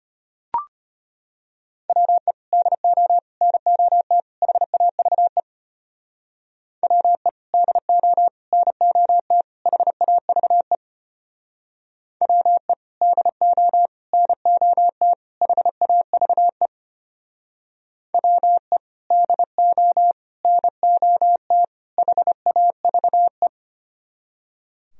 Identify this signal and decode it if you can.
Morse code, heard as a keyed tone, about 31 wpm, 700 Hz. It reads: WE DO NOT HAVE WE DO NOT HAVE WE DO NOT HAVE WE DO NOT HAVE
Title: Speed-Racing - Sets of 4 Words 25wpm